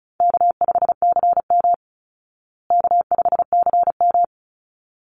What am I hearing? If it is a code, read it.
Morse code, heard as a keyed tone, about 34 wpm, 700 Hz. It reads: K5CM K5CM
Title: Call Signs Repeated 3x - 35wpm